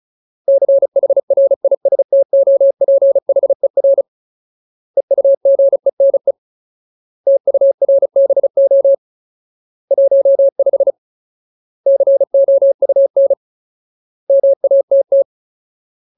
Morse code, heard as a keyed tone, about 34 wpm, 550 Hz. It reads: CHRISTOPHER EUGENE TURBO 15 COUN MATT
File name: Ultimate Word List / 35WPM Ultimate